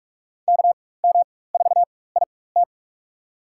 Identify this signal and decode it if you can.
Morse code, heard as a keyed tone, about 47 wpm, spaced as for 21 wpm, 700 Hz. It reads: KM4IT
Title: USA Call Signs - 4x Character Spacing - 45wpm